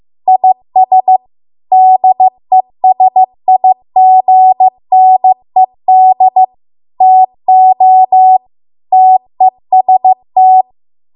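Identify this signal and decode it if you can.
Morse code, heard as a keyed tone, about 15 wpm, 750 Hz. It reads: IS DESIGNED TO TEST
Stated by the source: Morse code practice at 15 wpm from the national amateur radio association's code archive.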